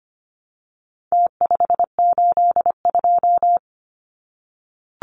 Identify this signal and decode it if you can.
Morse code, heard as a keyed tone, about 25 wpm, 700 Hz. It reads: T582
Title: Numbers - Contest Style - 4-Digit 25wpm